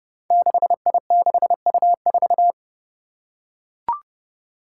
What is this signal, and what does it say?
Morse code, heard as a keyed tone, about 30 wpm, 700 Hz. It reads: 6I6U4
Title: Five Challenging Letter-Number Combinations 30wpm